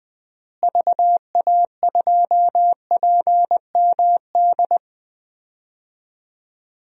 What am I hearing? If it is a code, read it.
Morse code, heard as a keyed tone, about 20 wpm, 700 Hz. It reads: VA2PMD
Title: DXCC-100 Call Signs - 6 Characters - Repeated 3x - 20wpm